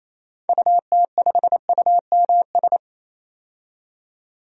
Morse code, heard as a keyed tone, about 28 wpm, 700 Hz. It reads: UT5UMS
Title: DXCC-100 Call Signs - 6 Characters - 28wpm